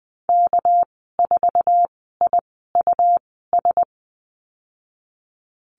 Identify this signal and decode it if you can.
Morse code, heard as a keyed tone, about 19 wpm, spaced as for 14 wpm, 700 Hz. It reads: K4IUS
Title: 219 Lesson 35g - Number 0 - Call Signs 20wpm